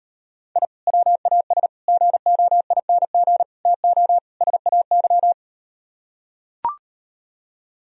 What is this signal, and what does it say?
Morse code, heard as a keyed tone, about 38 wpm, 700 Hz. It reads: I WAS GOING TO SAY